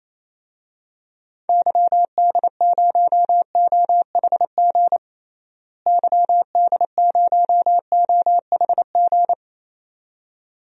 Morse code, heard as a keyed tone, about 27 wpm, 700 Hz. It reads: YD0OHG YD0OHG
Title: DXCC-100 Call Signs - 6 Characters - Repeated 2x - 28wpm